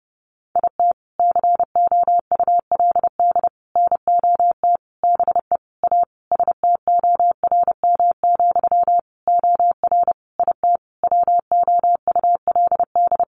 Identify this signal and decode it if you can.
Morse code, heard as a keyed tone, about 30 wpm, 700 Hz. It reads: IT COULD NOT BE A STORM, OR IT WOULD
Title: Sentences from Top 1500 Words 30wpm